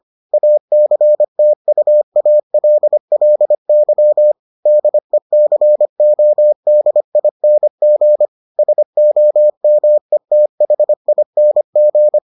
Morse code, heard as a keyed tone, about 25 wpm, 600 Hz. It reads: ACTUALLY DECODING SOMETHING